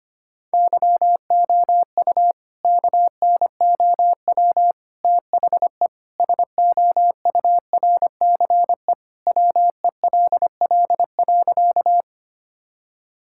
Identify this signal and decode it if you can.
Morse code, heard as a keyed tone, about 25 wpm, 700 Hz. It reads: YOU KNOW THE SOURCE WELL.
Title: Sentences from Top 1000 Words - Review Part 5 - 25wpm